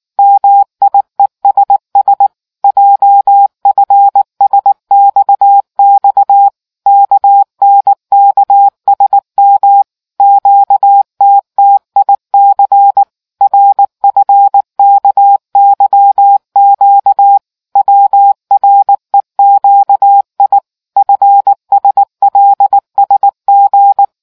Morse code, heard as a keyed tone, about 19 wpm, 800 Hz. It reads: MIESS JFSXX KNKSM QTTIC RFKYQ WREQI FSLSG